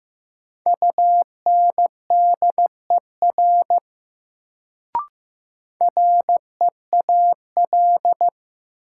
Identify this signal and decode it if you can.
Morse code, heard as a keyed tone, about 15 wpm, 700 Hz. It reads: UNDER REAL
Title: Sentences from Top 300 Words - New Word Drill - 15wpm